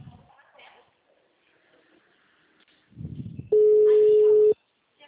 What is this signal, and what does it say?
Ringback tone (Europe, 425 Hz), with speech.